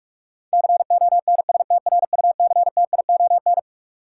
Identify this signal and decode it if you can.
Morse code, heard as a keyed tone, about 46 wpm, 700 Hz. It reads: CONSTRUCTION
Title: Sentences from Top 1600 Words - New Word Drill - 45wpm